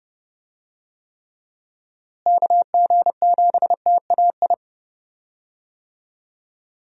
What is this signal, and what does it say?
Morse code, heard as a keyed tone, about 30 wpm, 700 Hz. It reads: KG7TAI
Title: USA Call Signs - 6 Characters - Repeated 3x - 30wpm